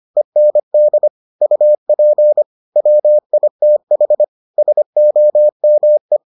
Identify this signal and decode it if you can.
Morse code, heard as a keyed tone, about 25 wpm, 600 Hz. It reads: END UP WITH SOME